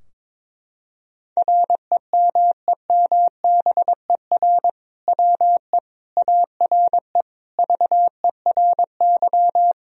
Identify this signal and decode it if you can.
Morse code, heard as a keyed tone, about 22 wpm, 700 Hz. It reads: REMEMBER WE ARE VERY
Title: Sentences from Top 1000 Words - Review Part 3 - 22wpm